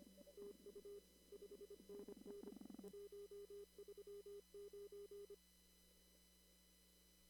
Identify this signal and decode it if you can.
Morse code, heard as a keyed tone, about 25 wpm, 400 Hz. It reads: TU 5NN 139
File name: CW / kt5j wpx cw-op-k5pi